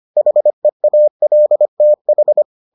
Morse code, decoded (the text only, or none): HEALTH